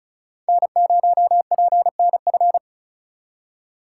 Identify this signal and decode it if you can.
Morse code, heard as a keyed tone, about 34 wpm, 700 Hz. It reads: N0PNF
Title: USA Call Signs - 5 Characters - 4x Character Spacing - 35wpm